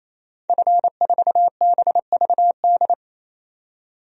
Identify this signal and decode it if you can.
Morse code, heard as a keyed tone, about 28 wpm, 700 Hz. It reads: F4BVD
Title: DXCC-100 Call Signs - 5 Characters - 28wpm